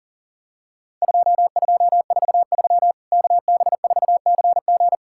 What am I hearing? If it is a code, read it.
Morse code, heard as a keyed tone, about 40 wpm, 700 Hz. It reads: 2243 KB4CG
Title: Worked All Europe DX Contest - QTC Exchange - 40wpm